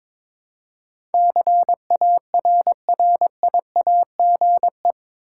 Morse code, heard as a keyed tone, about 22 wpm, 700 Hz. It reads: CARRIAGE
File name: practice / 22wpm / Eight Letter Words - 22wpm